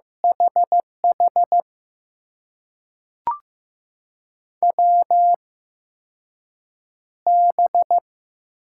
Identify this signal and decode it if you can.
Morse code, heard as a keyed tone, about 15 wpm, spaced as for 6 wpm, 700 Hz. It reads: HH WB
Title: USA Call Signs - 6 Characters - 8x Character Spacing - 15wpm